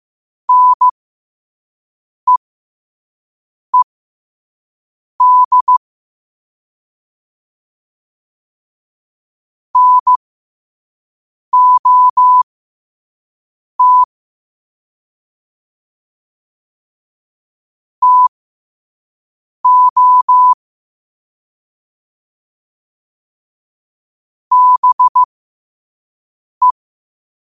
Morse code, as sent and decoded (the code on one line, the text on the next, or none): -. . -.. / -. --- - / - --- / -... .
NEED NOT TO BE